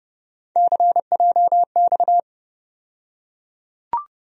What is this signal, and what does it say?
Morse code, heard as a keyed tone, about 32 wpm, 700 Hz. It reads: CJX
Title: USA Call Sign Postfixes - 3 Letters 30wpm